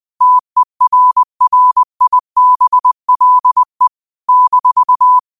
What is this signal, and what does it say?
Morse code, heard as a keyed tone, about 20 wpm, 1000 Hz. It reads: TERRIBLE -
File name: qotdfiles / 20wpm